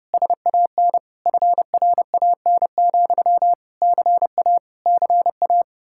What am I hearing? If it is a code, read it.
Morse code, heard as a keyed tone, about 29 wpm, 700 Hz. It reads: SAN FRAN, CA CA